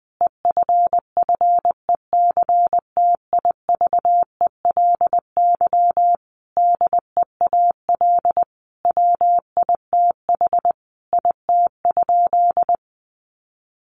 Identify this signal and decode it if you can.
Morse code, heard as a keyed tone, about 20 wpm, 700 Hz. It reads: EFFECTIVELY DEAL WITH IT?
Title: Sentences from Top 1400 Words 20wpm